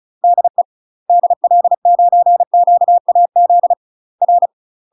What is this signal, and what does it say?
Morse code, heard as a keyed tone, about 36 wpm, 700 Hz. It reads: DE DL9QAZ R
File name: UK MASTER QSOs TRAINING / 35WPM / UK MASTER QSOs 0032